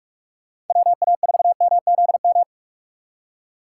Morse code, heard as a keyed tone, about 44 wpm, 700 Hz. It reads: WA4MZM
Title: USA Call Signs - 6 Characters - 6x Character Spacing - 45wpm